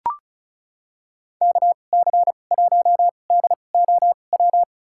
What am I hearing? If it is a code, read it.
Morse code, heard as a keyed tone, about 36 wpm, spaced as for 25 wpm, 700 Hz. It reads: KC1DOW
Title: USA Call Signs - 2x Character Spacing - 35wpm